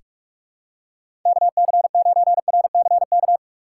Morse code, heard as a keyed tone, about 45 wpm, 700 Hz. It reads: KC9RCK
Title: USA Call Signs - 6 Characters - 3x Character Spacing - 45wpm